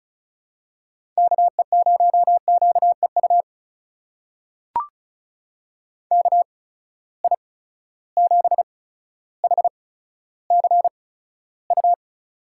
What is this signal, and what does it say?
Morse code, heard as a keyed tone, about 36 wpm, spaced as for 15 wpm, 700 Hz. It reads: KE0QEU KI7HCU
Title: USA Call Signs - 6 Characters - 8x Character Spacing - 35wpm